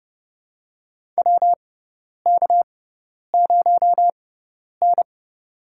Morse code, heard as a keyed tone, about 30 wpm, spaced as for 10 wpm, 700 Hz. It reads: WK0N
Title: USA Call Signs - 6x Character Spacing - 30wpm